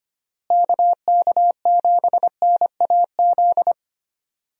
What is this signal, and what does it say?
Morse code, heard as a keyed tone, about 25 wpm, 700 Hz. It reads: KK7NAZ
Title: USA Call Signs - 6 Characters - Repeated 3x - 25wpm